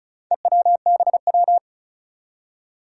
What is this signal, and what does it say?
Morse code, heard as a keyed tone, about 35 wpm, 700 Hz. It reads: EWBW
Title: Four Challenging Letter-Number Combinations 35wpm